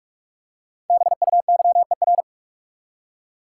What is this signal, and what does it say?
Morse code, heard as a keyed tone, about 46 wpm, 700 Hz. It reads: BUYER
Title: Five Letter Words - 45wpm